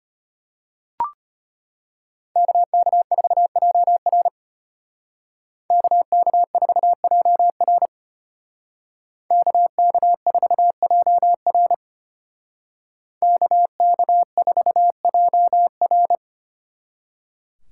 Morse code, heard as a keyed tone, about 33 wpm, 700 Hz. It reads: KK4JR KK4JR KK4JR KK4JR